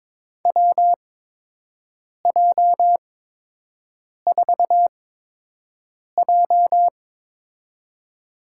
Morse code, heard as a keyed tone, about 22 wpm, spaced as for 6 wpm, 700 Hz. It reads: WJ4J
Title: USA Call Signs - 8x Character Spacing - 22wpm